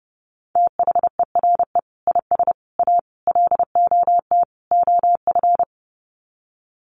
Morse code, heard as a keyed tone, about 30 wpm, 700 Hz. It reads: THERE IS A LOT OF
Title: Sets of 5 Words 30wpm